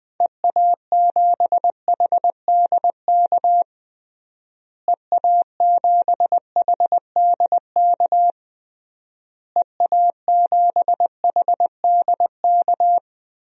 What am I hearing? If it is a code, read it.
Morse code, heard as a keyed tone, about 20 wpm, 700 Hz. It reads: EA7HDK EA7HDK EA7HDK
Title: DXCC-100 Call Signs - 6 Characters - Repeated 3x - 20wpm